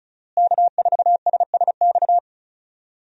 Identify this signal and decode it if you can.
Morse code, heard as a keyed tone, about 36 wpm, 700 Hz. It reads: K4SSX
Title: USA Call Signs - 5 Characters - Repeated 3x - 35wpm